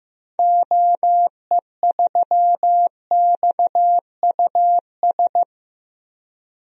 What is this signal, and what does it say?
Morse code, heard as a keyed tone, about 15 wpm, 700 Hz. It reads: OE3XUS
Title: DXCC-100 Call Signs - 6 Characters - 15wpm